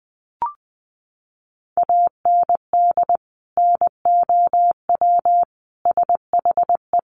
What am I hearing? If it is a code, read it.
Morse code, heard as a keyed tone, about 20 wpm, 700 Hz. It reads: AND NOW SHE
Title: Sentences from Top 1500 Words 20wpm